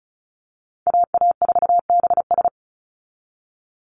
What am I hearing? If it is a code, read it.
Morse code, heard as a keyed tone, about 34 wpm, 700 Hz. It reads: AA4BS
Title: USA Call Signs 35wpm